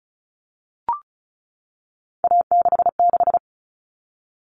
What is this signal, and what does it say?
Morse code, heard as a keyed tone, about 34 wpm, 700 Hz. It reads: A66
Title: All Cut Numbers - 3-Digit 35wpm